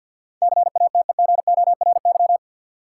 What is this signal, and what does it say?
Morse code, heard as a keyed tone, about 48 wpm, 700 Hz. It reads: CATEGORY
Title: Sentences from Top 1000 Words - Review Part 7 - New Word Drill - 50wpm